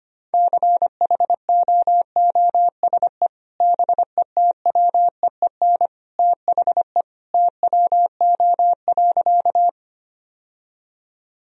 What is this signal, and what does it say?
Morse code, heard as a keyed tone, about 25 wpm, 700 Hz. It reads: CHOOSE BETWEEN THE TWO.
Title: Sentences from Top 1000 Words - Review Part 4 - 25wpm